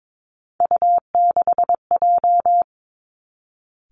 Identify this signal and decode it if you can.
Morse code, heard as a keyed tone, about 22 wpm, 700 Hz. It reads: U6J
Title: Three Challenging Letter-Number Combinations 22wpm